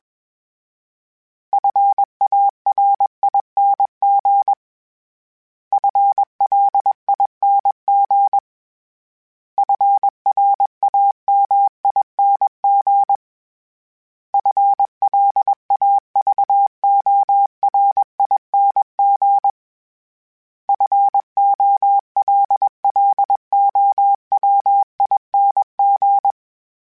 Morse code, heard as a keyed tone, about 21 wpm, 800 Hz. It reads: FARING FLING FRAMING FLAVORING FOLLOWING
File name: Special INGs / Special INGs-mp3s / Sngl Ing @20